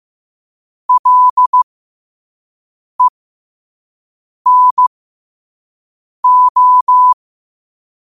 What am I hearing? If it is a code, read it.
Morse code, heard as a keyed tone, about 15 wpm, spaced as for 5 wpm, 1000 Hz. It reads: LENO
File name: qotdfiles / 5wpm